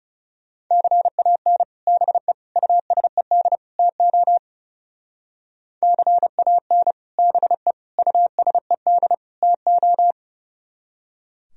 Morse code, heard as a keyed tone, about 37 wpm, 690 Hz. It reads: CAN BE USED TO CAN BE USED TO
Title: Speed-Racing - Sets of 4 Words 30wpm